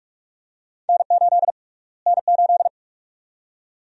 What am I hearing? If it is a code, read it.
Morse code, heard as a keyed tone, about 47 wpm, 700 Hz. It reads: N8 N8
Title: USA Call Sign Prefixes - Repeated 2x - 45wpm